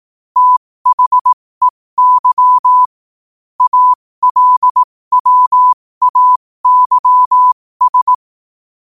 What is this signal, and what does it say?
Morse code, heard as a keyed tone, about 18 wpm, spaced as for 15 wpm, 1000 Hz. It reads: THEY ALWAYS
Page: 15 WPM morse code quotes